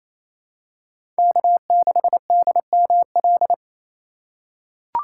Morse code, heard as a keyed tone, about 28 wpm, 700 Hz. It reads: K6DML
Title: USA Call Signs - 4x Character Spacing - 28wpm